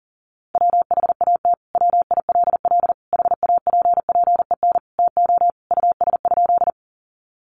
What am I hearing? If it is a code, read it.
Morse code, heard as a keyed tone, about 40 wpm, 700 Hz. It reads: WHAT WILL HAPPEN TO US?
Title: Sentences from Top 400 Words 40wpm